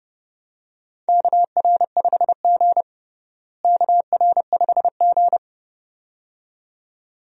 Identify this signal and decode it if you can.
Morse code, heard as a keyed tone, about 31 wpm, 700 Hz. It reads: KR5G KR5G
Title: USA Call Signs - 4 Characters - Repeated 2x - 30wpm